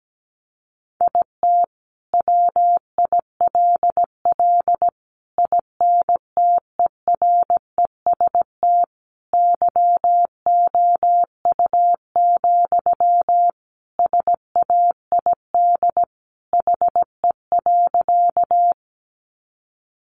Morse code, heard as a keyed tone, about 17 wpm, 700 Hz. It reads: IT WILL INTEREST YOU, SAID HE.
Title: Sentences from Top 400 Words 17wpm